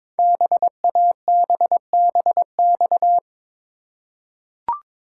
Morse code, heard as a keyed tone, about 22 wpm, 700 Hz. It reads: BABBX
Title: Code Groups - 5 Letters - 22wpm